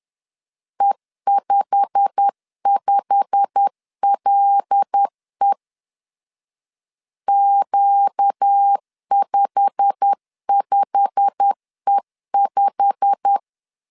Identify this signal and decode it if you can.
Morse code, heard as a keyed tone, about 10 wpm, 800 Hz. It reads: E55LE Q55E5
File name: LEZIONI / 8 YJ45 / 8 YJ45 10wpm 2549